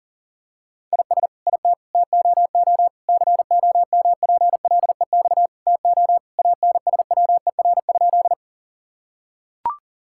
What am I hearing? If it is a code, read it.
Morse code, heard as a keyed tone, about 40 wpm, 690 Hz. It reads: IS IT TOO COMPLEX TO ANSWER?
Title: Sentences from Top 1000 Words - Review Part 6 - 40wpm